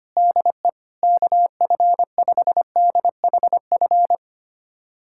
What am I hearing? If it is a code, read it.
Morse code, heard as a keyed tone, about 25 wpm, 700 Hz. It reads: DE KF5DHF
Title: USA Callsign DE Callsign - 6 Characters - 25wpm